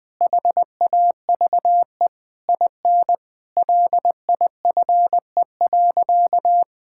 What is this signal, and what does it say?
Morse code, heard as a keyed tone, about 20 wpm, 700 Hz. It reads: HAVE IN LIFE.
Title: Sentences from Top 1000 Words - Review Part 6 - 20wpm